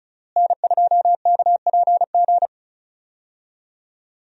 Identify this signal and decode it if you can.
Morse code, heard as a keyed tone, about 36 wpm, 700 Hz. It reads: N2KPG
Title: USA Call Signs - 5 Characters - Repeated 2x - 35wpm